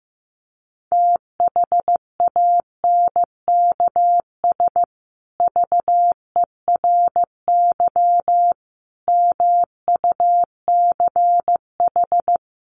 Morse code, heard as a keyed tone, about 15 wpm, 700 Hz. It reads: THANKS VERY MUCH
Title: Sets of 5 Words Encore 15wpm